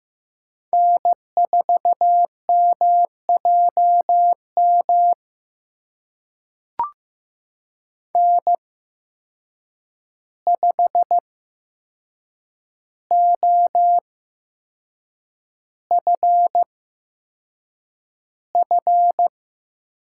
Morse code, heard as a keyed tone, about 15 wpm, spaced as for 6 wpm, 700 Hz. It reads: N4MJM N5OFF
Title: USA Call Signs - 8x Character Spacing - 15wpm